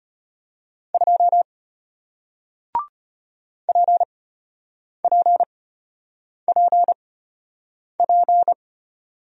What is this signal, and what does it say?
Morse code, heard as a keyed tone, about 33 wpm, spaced as for 8 wpm, 700 Hz. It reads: 2 PPPP